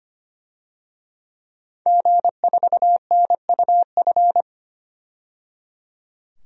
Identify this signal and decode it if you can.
Morse code, heard as a keyed tone, about 25 wpm, 700 Hz. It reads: G4NUF
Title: DXCC-100 Call Signs - 5 Characters - 25wpm